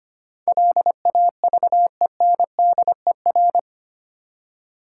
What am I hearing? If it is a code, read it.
Morse code, heard as a keyed tone, about 25 wpm, 700 Hz. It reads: LAVENDER
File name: practice / 25wpm / Difficult Words with Hints - 25wpm